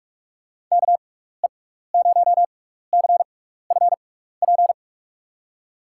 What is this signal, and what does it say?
Morse code, heard as a keyed tone, about 44 wpm, spaced as for 15 wpm, 700 Hz. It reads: KE0CFP
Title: USA Call Signs - 6x Character Spacing - 45wpm